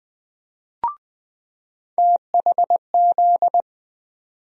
Morse code, heard as a keyed tone, about 20 wpm, 700 Hz. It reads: THZ
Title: USA Call Sign Postfixes - 3 Letters 20wpm